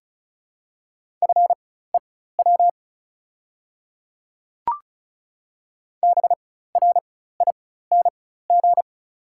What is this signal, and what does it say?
Morse code, heard as a keyed tone, about 36 wpm, spaced as for 16 wpm, 700 Hz. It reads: FEW BRING
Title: 4x Character Spacing - Top 500 Words - 35wpm